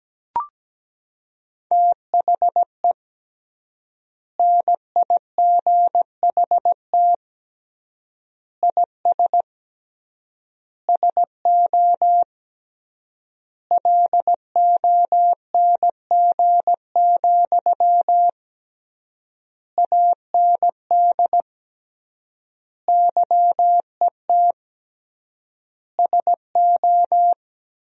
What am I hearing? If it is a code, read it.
Morse code, heard as a keyed tone, about 17 wpm, 700 Hz. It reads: THE NIGHT IS SO LONG, AND YET SO
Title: 3x Word-Spacing - Sentences from Top 500 Words - 17wpm